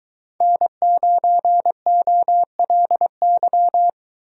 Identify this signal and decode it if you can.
Morse code, heard as a keyed tone, about 22 wpm, 700 Hz. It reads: N9OLY